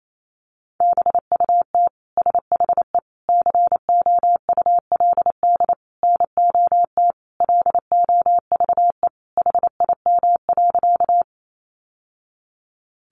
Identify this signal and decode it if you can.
Morse code, heard as a keyed tone, about 28 wpm, 700 Hz. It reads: BUT SHE COULD NOT LOVE HIM.
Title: Sentences from Top 600 Words 28wpm